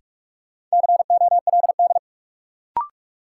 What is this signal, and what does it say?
Morse code, heard as a keyed tone, about 46 wpm, 700 Hz. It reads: COLD
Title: Sets of 5 Words - New Word Drill - 45wpm